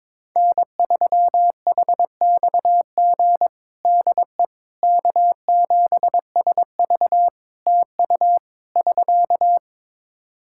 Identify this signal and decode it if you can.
Morse code, heard as a keyed tone, about 22 wpm, 700 Hz. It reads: N3HXG DE K7SV TU <SK>
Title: K1USN SST QSO Practice 22wpm